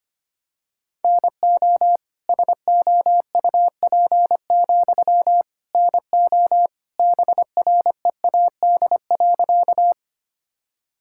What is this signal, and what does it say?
Morse code, heard as a keyed tone, about 25 wpm, 700 Hz. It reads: NO SOUP, NO BREAD.
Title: Sentences from Top 1600 Words 25wpm